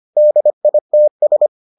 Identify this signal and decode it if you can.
Morse code, heard as a keyed tone, about 25 wpm, 600 Hz. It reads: DITS